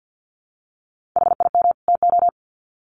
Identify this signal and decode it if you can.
Morse code, heard as a keyed tone, about 50 wpm, 700 Hz. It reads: HIM TO
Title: Sets of 2 Words 50wpm